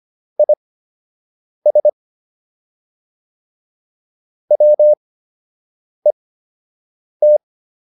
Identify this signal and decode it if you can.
Morse code, heard as a keyed tone, about 25 wpm, spaced as for 7 wpm, 600 Hz. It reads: IS WET